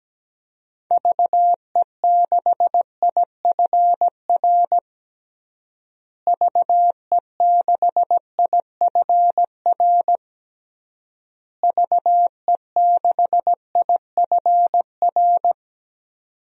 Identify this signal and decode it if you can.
Morse code, heard as a keyed tone, about 17 wpm, 700 Hz. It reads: VE6IFR VE6IFR VE6IFR
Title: DXCC-100 Call Signs - 6 Characters - Repeated 3x - 17wpm